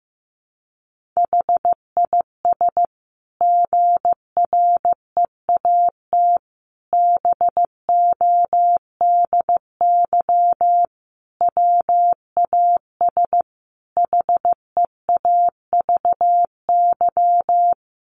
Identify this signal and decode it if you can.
Morse code, heard as a keyed tone, about 15 wpm, 700 Hz. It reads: HIS GREAT BODY WAS HEAVY
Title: Sentences from Top 1400 Words 15wpm